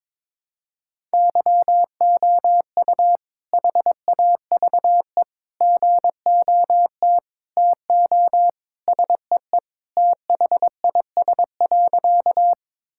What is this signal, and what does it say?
Morse code, heard as a keyed tone, about 22 wpm, 700 Hz. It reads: YOU HAVE GOT TO SEE THIS.
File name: practice / 22wpm / Sentences with 2-Word Phrases - 22wpm